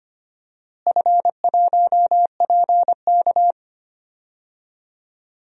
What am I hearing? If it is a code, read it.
Morse code, heard as a keyed tone, about 25 wpm, 700 Hz. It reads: F1PK
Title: DXCC-100 Call Signs - 4 Characters - Repeated 3x - 25wpm